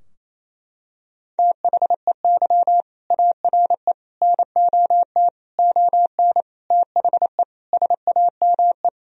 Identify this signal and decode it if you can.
Morse code, heard as a keyed tone, about 28 wpm, 700 Hz. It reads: THEY ARE NOT ON THE SAME
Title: Sentences from Top 1000 Words - Review Part 4 - 28wpm